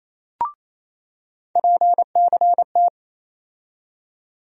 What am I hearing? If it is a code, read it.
Morse code, heard as a keyed tone, about 28 wpm, 700 Hz. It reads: PCT
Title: USA Call Sign Postfixes - 3 Letters 28wpm